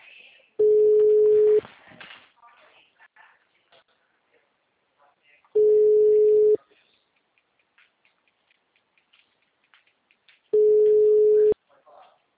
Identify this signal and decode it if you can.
Ringback tone (Europe, 425 Hz), with speech.